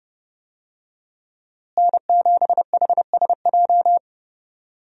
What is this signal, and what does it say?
Morse code, heard as a keyed tone, about 30 wpm, 700 Hz. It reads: N7HSJ
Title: USA Call Signs - 5 Characters - Repeated 3x - 30wpm